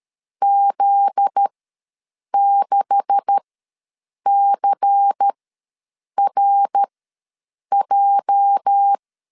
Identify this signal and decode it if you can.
Morse code, heard as a keyed tone, about 13 wpm, spaced as for 7 wpm, 800 Hz. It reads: Z6CRJ
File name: LEZIONI / 9 286 ws 12wpm 3351